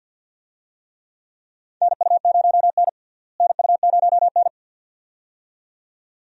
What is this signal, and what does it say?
Morse code, heard as a keyed tone, about 47 wpm, 700 Hz. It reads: NU0N NU0N
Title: USA Call Signs - 4 Characters - Repeated 2x - 50wpm